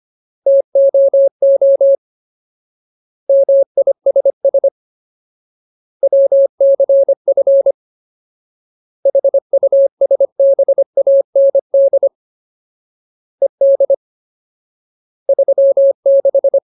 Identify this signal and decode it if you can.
Morse code, heard as a keyed tone, about 25 wpm, 550 Hz. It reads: TOO MISS WCF HUSBAND ED 36
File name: Ultimate Word List / 25WPM Ultimate